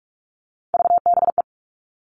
Morse code, heard as a keyed tone, about 44 wpm, 700 Hz. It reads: VBE
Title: All Cut Numbers - 3-Digit 45wpm